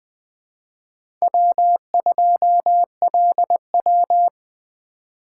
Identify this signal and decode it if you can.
Morse code, heard as a keyed tone, about 20 wpm, 700 Hz. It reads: W2LW
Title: USA Call Signs - 3x Character Spacing - 20wpm